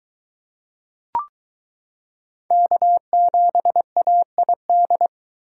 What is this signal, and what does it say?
Morse code, heard as a keyed tone, about 22 wpm, 700 Hz. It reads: K7AID